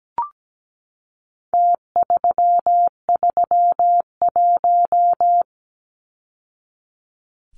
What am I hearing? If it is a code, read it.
Morse code, heard as a keyed tone, about 17 wpm, 700 Hz. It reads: T331
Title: Numbers - Contest Style - 4-Digit 17wpm